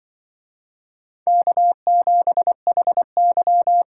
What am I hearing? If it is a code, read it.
Morse code, heard as a keyed tone, about 23 wpm, 700 Hz. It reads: K7HY
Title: Speed Racing - USA 1x2 Call Signs 20wpm